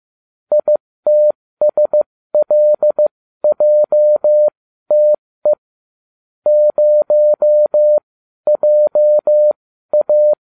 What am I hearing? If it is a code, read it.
Morse code, heard as a keyed tone, about 15 wpm, spaced as for 12 wpm, 600 Hz. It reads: ITSLJTE 0JA